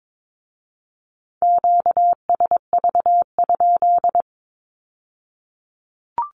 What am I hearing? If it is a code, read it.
Morse code, heard as a keyed tone, about 22 wpm, 700 Hz. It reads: QSV?
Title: Q-Codes with Spoken Meaning - 22wpm